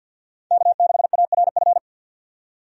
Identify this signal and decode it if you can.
Morse code, heard as a keyed tone, about 52 wpm, 700 Hz. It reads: K6ARF
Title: USA Call Signs - 5 Characters - 50wpm